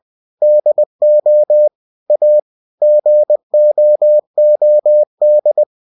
Morse code, heard as a keyed tone, about 20 wpm, 600 Hz. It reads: DO A GOOD
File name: cw-books / winnie-the-pooh / winnie-20wpm-page-0092